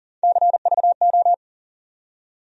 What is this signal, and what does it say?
Morse code, heard as a keyed tone, about 40 wpm, 700 Hz. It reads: CVO
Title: USA Call Sign Postfixes - 3 Letters 40wpm